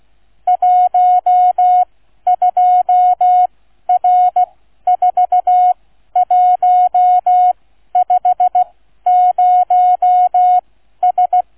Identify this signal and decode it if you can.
Morse code, heard as a keyed tone, about 16 wpm, spaced as for 12 wpm, 730 Hz. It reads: 12R4150S
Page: QRS International Flight Contest 2025 Demo Flight NR 00 Sound files Listen-to or download incoming QNC sound file